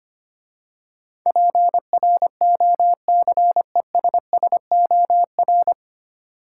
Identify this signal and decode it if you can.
Morse code, heard as a keyed tone, about 25 wpm, 700 Hz. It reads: PROCESSOR
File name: practice / 25wpm / 9 Letter Words - First 3 Letters Given - 25wpm